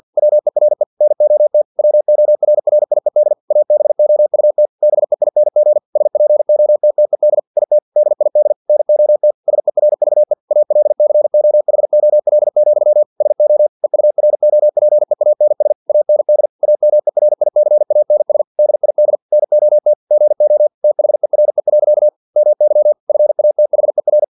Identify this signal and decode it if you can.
Morse code, heard as a keyed tone, about 50 wpm, 600 Hz. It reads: WERE NOT WORRIED ABOUT BEING SPOTTED IT DID NOT SERVE ALCOHOL, SO EUROPEANS AND AMERICANS DID NOT GO THERE. MY FATHER
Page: Never 49wpm